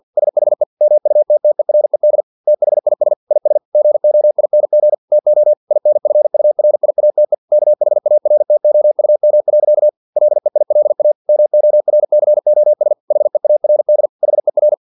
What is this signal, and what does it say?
Morse code, heard as a keyed tone, about 50 wpm, 600 Hz. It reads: SHE MUTTERED THIS IS GOING TO INFURIATE KHARTOUM. LEILA MORCOS HEARD HER
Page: Never 49wpm